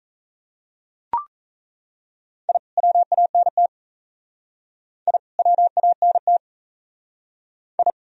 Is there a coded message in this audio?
Yes: Morse code from 2.5 to 7.9 s, heard as a keyed tone, about 37 wpm, 700 Hz. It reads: I WANT I WANT I